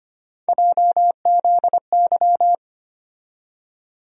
Morse code, heard as a keyed tone, about 25 wpm, 700 Hz. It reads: JZY
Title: USA Call Sign Postfixes - 3 Letters 25wpm